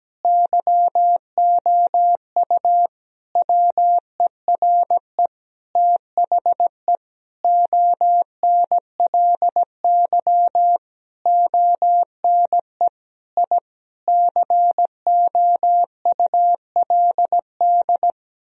Morse code, heard as a keyed tone, about 17 wpm, 700 Hz. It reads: YOU WERE THE ONLY ONE I COULD
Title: Sentences from Top 1000 Words - Review Part 4 - 17wpm